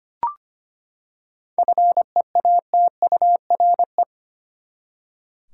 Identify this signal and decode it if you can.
Morse code, heard as a keyed tone, about 25 wpm, 700 Hz. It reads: FEATURE
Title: Seven Letter Words - 25wpm